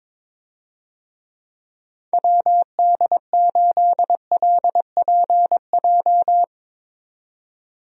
Morse code, heard as a keyed tone, about 22 wpm, 700 Hz. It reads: WD8LPJ
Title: USA Call Signs - 6 Characters - 22wpm